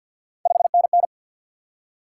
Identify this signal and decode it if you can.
Morse code, heard as a keyed tone, about 53 wpm, 700 Hz. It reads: 5NN
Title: Top 90 QSO Elements 50wpm